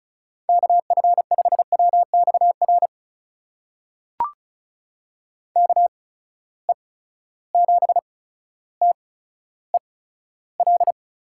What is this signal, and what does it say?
Morse code, heard as a keyed tone, about 35 wpm, spaced as for 15 wpm, 700 Hz. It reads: KF5WXR KE7TEL
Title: USA Call Signs - 6 Characters - 8x Character Spacing - 35wpm